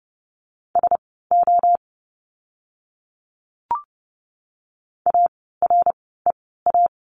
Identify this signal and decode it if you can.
Morse code, heard as a keyed tone, about 30 wpm, spaced as for 17 wpm, 700 Hz. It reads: SO AREA
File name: course / 30:10wpm / 115 Lesson 19f - Letter w - Words 30wpm